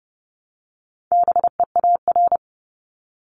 Morse code, heard as a keyed tone, about 30 wpm, 700 Hz. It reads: BEAR
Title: Multi-letter Phonemes 30wpm